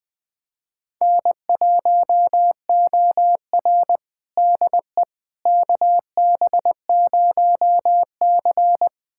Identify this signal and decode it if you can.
Morse code, heard as a keyed tone, about 20 wpm, 700 Hz. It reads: N1OR DE KB0C
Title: USA Callsign DE Callsign - 4 Characters - 20wpm